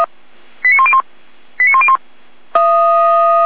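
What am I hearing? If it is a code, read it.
Five-tone selective call (ZVEI), heard as quick runs of five tones, tones of about 70 ms. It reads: 89101 89101
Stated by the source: ZVEI five-tone call 89052 received on an SDR.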